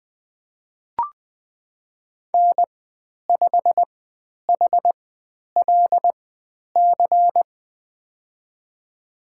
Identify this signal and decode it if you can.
Morse code, heard as a keyed tone, about 20 wpm, spaced as for 10 wpm, 700 Hz. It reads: N5HLC
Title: 095 Lesson 16g - Letter c - Call Signs 20wpm